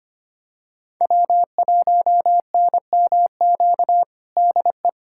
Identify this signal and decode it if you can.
Morse code, heard as a keyed tone, about 25 wpm, 700 Hz. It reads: W1NMQ DE
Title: USA Callsign DE Callsign 25wpm